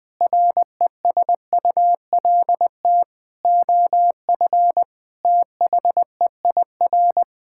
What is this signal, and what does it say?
Morse code, heard as a keyed tone, about 20 wpm, 700 Hz. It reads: RESULT OF THEIR
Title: Sentences from Top 1000 Words - Review Part 5 - 20wpm